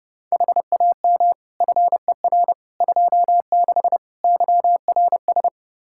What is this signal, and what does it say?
Morse code, heard as a keyed tone, about 29 wpm, 700 Hz. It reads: HAM FER 26 YRS
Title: Practice QSO - Exchange 3 - More Info - 30wpm